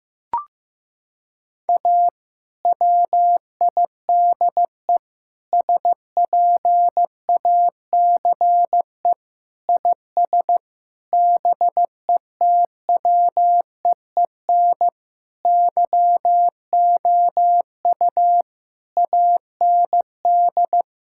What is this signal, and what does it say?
Morse code, heard as a keyed tone, about 14 wpm, 700 Hz. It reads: A WIDE SPACE IS BETWEEN YOU AND
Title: Sentences from Top 1000 Words - Review Part 6 - 15wpm